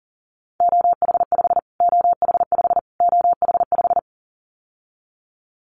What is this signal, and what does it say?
Morse code, heard as a keyed tone, about 40 wpm, 700 Hz. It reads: OH5 OH5 OH5
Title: Callsign Prefixes for Top 25 DXCC - Repeated 3x - 40wpm